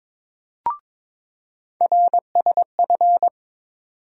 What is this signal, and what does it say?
Morse code, heard as a keyed tone, about 22 wpm, 700 Hz. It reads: RSF
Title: USA Call Sign Postfixes - 3 Letters 22wpm